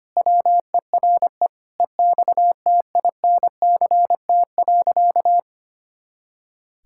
Morse code, heard as a keyed tone, about 25 wpm, 700 Hz. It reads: WERE EXTINCT.
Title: Pangram Challenge v2 - 25wpm